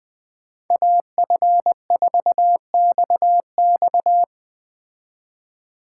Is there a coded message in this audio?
Yes: Morse code from 0.7 to 4.2 s, heard as a keyed tone, about 20 wpm, 700 Hz. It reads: AF4XX